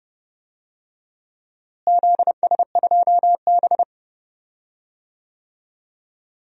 Morse code, heard as a keyed tone, about 30 wpm, 700 Hz. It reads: ZS2B
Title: DXCC-100 Call Signs - 4 Characters - Repeated 2x - 30wpm